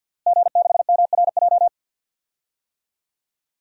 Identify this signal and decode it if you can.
Morse code, heard as a keyed tone, about 52 wpm, 700 Hz. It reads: G6MRJ
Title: DXCC-100 Call Signs - 5 Characters - Repeated 3x - 50wpm